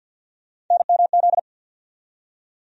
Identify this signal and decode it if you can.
Morse code, heard as a keyed tone, about 52 wpm, 700 Hz. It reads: NMZ
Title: USA Call Sign Postfixes - 3 Letters 50wpm